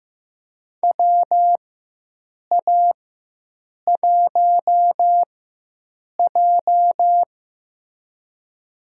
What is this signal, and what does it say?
Morse code, heard as a keyed tone, about 15 wpm, spaced as for 7 wpm, 700 Hz. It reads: WA1J
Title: USA Call Signs - 4 Characters - 4x Character Spacing - 15wpm